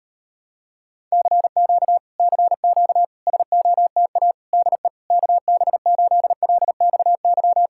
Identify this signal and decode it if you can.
Morse code, heard as a keyed tone, about 38 wpm, 700 Hz. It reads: CQ CQ SOTA DE KB8LXY